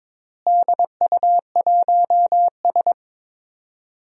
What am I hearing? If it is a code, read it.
Morse code, heard as a keyed tone, about 22 wpm, 700 Hz. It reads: DU1S
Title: DXCC-100 Call Signs - 4 Characters - Repeated 2x - 22wpm